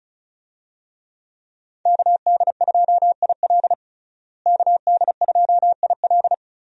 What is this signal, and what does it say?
Morse code, heard as a keyed tone, about 36 wpm, 700 Hz. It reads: KD2IL KD2IL
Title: USA Call Signs - 5 Characters - Repeated 2x - 35wpm